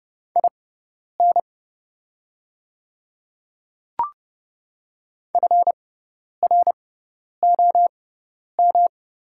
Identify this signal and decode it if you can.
Morse code, heard as a keyed tone, about 31 wpm, spaced as for 10 wpm, 700 Hz. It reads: IN FROM
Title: 6x Character Spacing - Top 100 Words - 30wpm